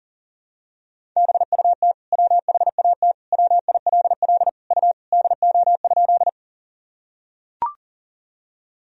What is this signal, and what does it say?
Morse code, heard as a keyed tone, about 40 wpm, 700 Hz. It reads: BUT WHAT WILL U DO?
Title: Sentences from Top 100 Words and Abbreviations 40wpm